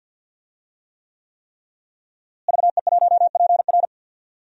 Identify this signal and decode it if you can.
Morse code, heard as a keyed tone, about 48 wpm, 700 Hz. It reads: VE1PR
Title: DXCC-100 Call Signs - 5 Characters - Repeated 3x - 50wpm